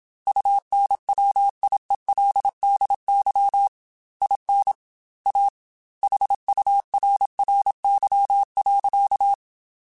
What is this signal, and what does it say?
Morse code, heard as a keyed tone, about 27 wpm, 800 Hz. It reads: UNWIELDY IN A HURRY.
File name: Technical Subjects / Technical Subjects-mp3s / AND Gate @25